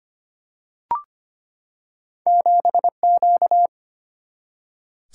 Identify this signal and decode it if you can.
Morse code, heard as a keyed tone, about 25 wpm, 700 Hz. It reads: 7 Q
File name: Two Letter-Number Combinations - 25wpm